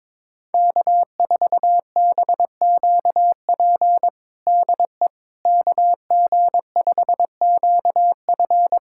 Morse code, heard as a keyed tone, about 22 wpm, 700 Hz. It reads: K4BQP DE KG5QF
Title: USA Callsign DE Callsign 22wpm